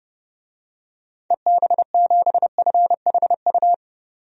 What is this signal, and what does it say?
Morse code, heard as a keyed tone, about 32 wpm, 700 Hz. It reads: EB7FHU
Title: DXCC-100 Call Signs - 6 Characters - 30wpm